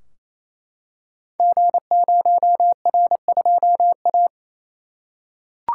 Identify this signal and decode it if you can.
Morse code, heard as a keyed tone, about 27 wpm, 700 Hz. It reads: G0R2A
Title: Code Groups - 5 Letter:Numbers - 28wpm